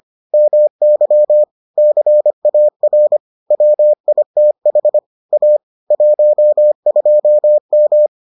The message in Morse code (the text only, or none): MY CAR WITH A 12M